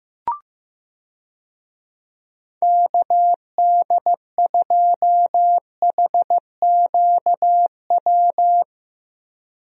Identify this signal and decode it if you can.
Morse code, heard as a keyed tone, about 16 wpm, 700 Hz. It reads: KD2HQW
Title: USA Call Signs - 6 Characters - Repeated 2x - 15wpm